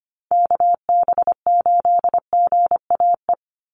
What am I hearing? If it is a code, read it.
Morse code, heard as a keyed tone, about 25 wpm, 700 Hz. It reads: KB8GAE
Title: Call Signs - 25wpm